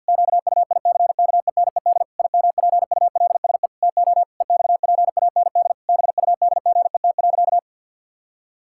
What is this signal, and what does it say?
Morse code, heard as a keyed tone, about 53 wpm, 700 Hz. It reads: QUICKENED IMPULSE TO EXPAND BUDGET.
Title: Pangram Challenge v2 - 50wpm